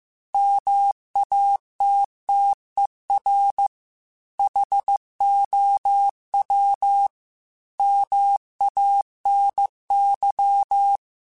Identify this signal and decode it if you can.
Morse code, heard as a keyed tone, about 15 wpm, 800 Hz. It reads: MATTER HOW MANY
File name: Technical Subjects / Technical Subjects-mp3s / OR Gate @13